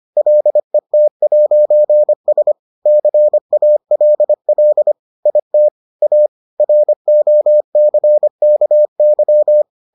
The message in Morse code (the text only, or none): LET'S CALL IT A ROCKY